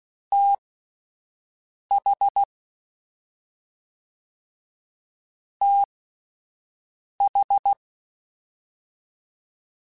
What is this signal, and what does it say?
Morse code, heard as a keyed tone, about 16 wpm, spaced as for 5 wpm, 800 Hz. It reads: TH TH